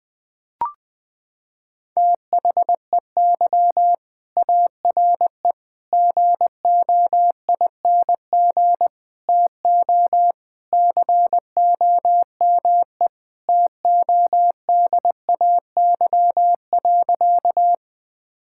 Morse code, heard as a keyed tone, about 20 wpm, 700 Hz. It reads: THEY ARE GOING TO COME TODAY.
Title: Sentences with 3-Word Phrases - 20wpm